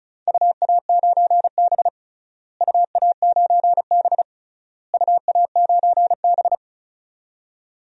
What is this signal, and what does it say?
Morse code, heard as a keyed tone, about 36 wpm, 690 Hz. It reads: UA9B UA9B UA9B
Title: DXCC-100 Call Signs - 4 Characters - Repeated 3x - 35wpm